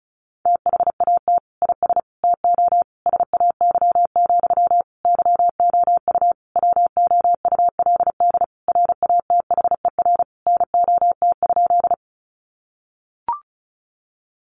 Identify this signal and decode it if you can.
Morse code, heard as a keyed tone, about 34 wpm, 700 Hz. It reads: THAT IS TO SAY, YOU WOULD RATHER NOT?
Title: Sentences from Top 700 Words 35wpm